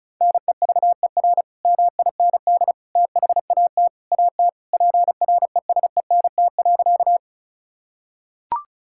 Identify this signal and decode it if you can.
Morse code, heard as a keyed tone, about 36 wpm, 700 Hz. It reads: NEVER MIND THAT AT PRESENT.
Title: Sentences from Top 1000 Words - Review Part 5 - 35wpm